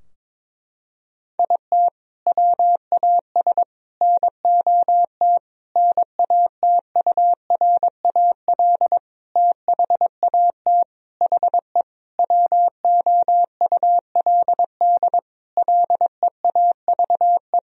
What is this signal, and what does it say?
Morse code, heard as a keyed tone, about 22 wpm, 700 Hz. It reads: IT WAS NOT NATURAL THAT HE WOULD LEAVE
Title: Sentences from Top 1000 Words - Review Part 4 - 22wpm